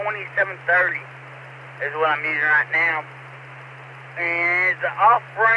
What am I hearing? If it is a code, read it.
Emergency Alert System (SAME) tones (header bursts), heard as harsh buzzing bursts, with speech. It is heard from 0.3 to 4.8 s.